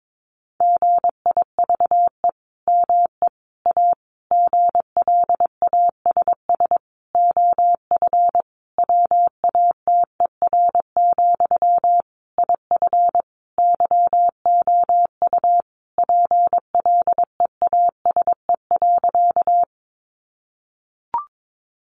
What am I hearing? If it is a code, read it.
Morse code, heard as a keyed tone, about 22 wpm, 700 Hz. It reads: GIVE ME A GLASS OF WATER, IF YOU PLEASE.
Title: Sentences from Top 1500 Words 22wpm